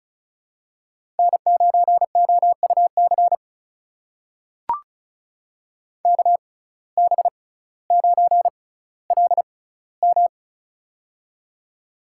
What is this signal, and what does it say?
Morse code, heard as a keyed tone, about 36 wpm, spaced as for 18 wpm, 700 Hz. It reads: N9OUC KB9LM
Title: USA Call Signs - 5 Characters - 6x Character Spacing - 35wpm